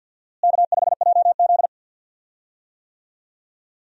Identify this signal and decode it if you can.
Morse code, heard as a keyed tone, about 52 wpm, 700 Hz. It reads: K5JZ
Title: USA Call Signs - 4 Characters - Repeated 2x - 50wpm